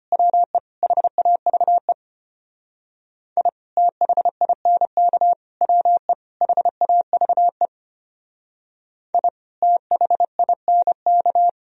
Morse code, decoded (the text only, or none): WE HAVE I THINK WE HAVE I THINK